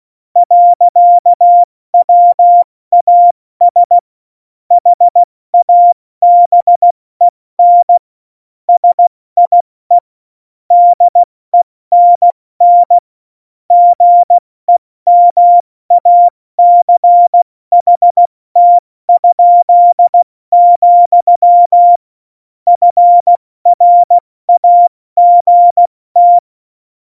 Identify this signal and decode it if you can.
Morse code, heard as a keyed tone, about 16 wpm, 700 Hz. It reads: .WAS HABEN SIE DENN GEMACHT?, FRAGT
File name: Morsekurs / Witze / Witz 11 August 2020, 70 F80